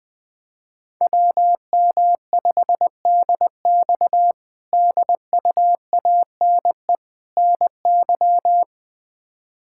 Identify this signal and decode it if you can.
Morse code, heard as a keyed tone, about 20 wpm, 700 Hz. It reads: WM5DX DUANE NY
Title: K1USN SST QSO Practice - 20wpm